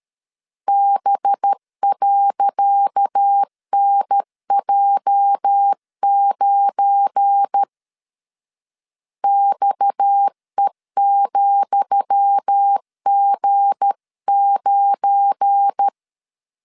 Morse code, heard as a keyed tone, about 13 wpm, 800 Hz. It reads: B.NJ9 XE,G9
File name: LEZIONI / 10 SIGNS / 10D COMA 12wpm 2544